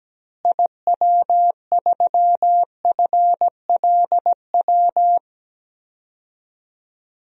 Morse code, heard as a keyed tone, about 17 wpm, 700 Hz. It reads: IW3FLW
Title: DXCC-100 Call Signs - 6 Characters - Repeated 3x - 17wpm